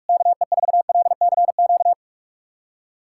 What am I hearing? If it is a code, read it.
Morse code, heard as a keyed tone, about 44 wpm, 700 Hz. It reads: KE4LCQ
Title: USA Callsign DE Callsign - 6 Characters - 45wpm